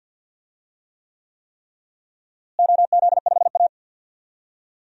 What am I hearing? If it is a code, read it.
Morse code, heard as a keyed tone, about 51 wpm, 700 Hz. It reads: OZ5A
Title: DXCC-100 Call Signs - 4 Characters - Repeated 3x - 50wpm